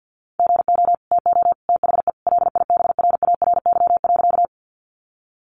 Morse code, heard as a keyed tone, about 50 wpm, 700 Hz. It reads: GO TO THE LIBRARY.
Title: Sentences from Top 600 Words 50wpm